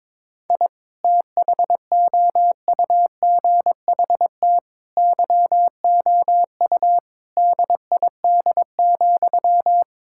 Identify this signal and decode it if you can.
Morse code, heard as a keyed tone, about 22 wpm, 700 Hz. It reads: I THOUGHT YOU DID,
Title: Sentences from Top 1000 Words - Review Part 5 - 22wpm